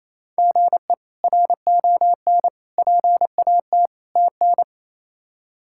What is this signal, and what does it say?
Morse code, heard as a keyed tone, about 28 wpm, 700 Hz. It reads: GE RON PAT TN
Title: K1USN SST QSO Practice - 28wpm